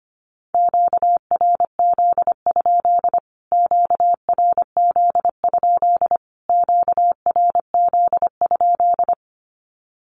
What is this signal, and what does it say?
Morse code, heard as a keyed tone, about 25 wpm, 700 Hz. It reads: QRZ? QRZ? QRZ?
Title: Q-Codes Repeated 3x 25wpm